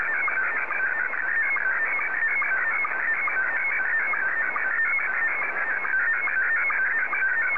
SSTV (slow-scan television), Scottie 1.
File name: sstv / S1-JA3QJQ